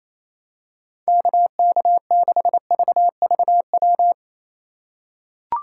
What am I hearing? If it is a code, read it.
Morse code, heard as a keyed tone, about 28 wpm, 700 Hz. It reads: KK6VVW
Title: USA Call Signs - 6 Characters - 4x Character Spacing - 28wpm